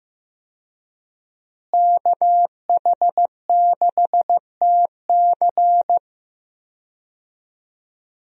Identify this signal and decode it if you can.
Morse code, heard as a keyed tone, about 15 wpm, 700 Hz. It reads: KH6TC
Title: DXCC-100 Call Signs - 5 Characters - Repeated 2x - 15wpm